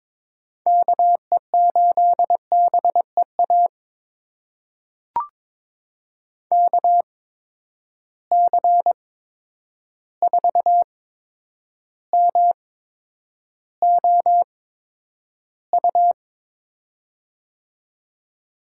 Morse code, heard as a keyed tone, about 22 wpm, spaced as for 9 wpm, 700 Hz. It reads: KE8BEA KC4MOU
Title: USA Call Signs - 6 Characters - 8x Character Spacing - 22wpm